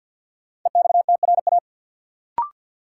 Morse code, heard as a keyed tone, about 53 wpm, 700 Hz. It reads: EXTRA
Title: Sentences from Top 1000 Words - New Word Drill - 50wpm